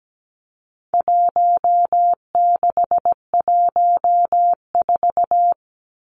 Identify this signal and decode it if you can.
Morse code, heard as a keyed tone, about 17 wpm, 700 Hz. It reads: 1614
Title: Numbers - Contest Style - 4-Digit 17wpm